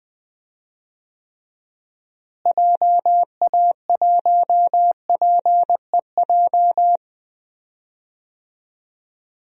Morse code, heard as a keyed tone, about 21 wpm, 700 Hz. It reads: JA1PEJ
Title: DXCC-100 Call Signs - 6 Characters - Repeated 3x - 20wpm